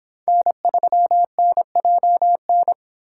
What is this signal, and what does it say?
Morse code, heard as a keyed tone, about 28 wpm, 700 Hz. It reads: N3NJN